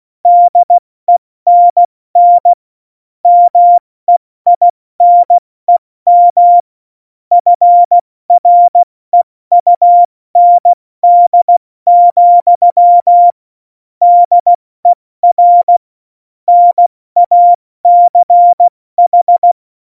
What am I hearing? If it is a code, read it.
Morse code, heard as a keyed tone, about 16 wpm, 700 Hz. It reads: DENN MEINEM FREUND, DER NACH
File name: Morsekurs / Witze / Witz 20. Oktober 2020, 70 F80